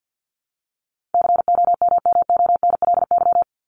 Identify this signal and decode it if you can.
Morse code, heard as a keyed tone, about 48 wpm, 700 Hz. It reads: COMMONLY
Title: LY Suffix - Top 100 Words with Base Words 50wpm